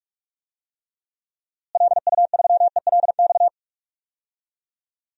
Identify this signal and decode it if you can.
Morse code, heard as a keyed tone, about 41 wpm, 700 Hz. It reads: LU3ELX
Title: DXCC-100 Call Signs - 6 Characters - Repeated 3x - 45wpm